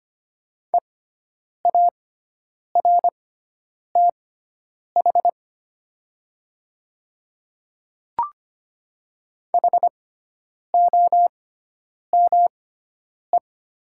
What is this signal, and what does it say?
Morse code, heard as a keyed tone, about 25 wpm, spaced as for 9 wpm, 700 Hz. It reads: EARTH HOME
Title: 6x Character Spacing - Top 200 Words - 25wpm